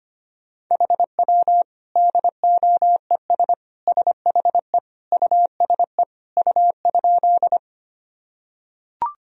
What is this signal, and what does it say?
Morse code, heard as a keyed tone, about 25 wpm, 700 Hz. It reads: HW DOES SHE USE U?
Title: Sentences from Top 200 Words and Abbreviations 25wpm